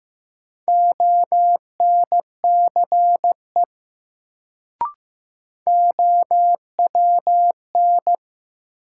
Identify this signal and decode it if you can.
Morse code, heard as a keyed tone, about 15 wpm, 700 Hz. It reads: ONCE OWN
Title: Sentences from Top 300 Words - New Word Drill - 15wpm